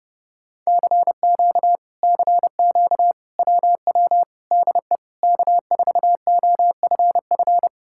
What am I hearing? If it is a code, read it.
Morse code, heard as a keyed tone, about 29 wpm, 700 Hz. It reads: CQ CQ WW DE K4OFF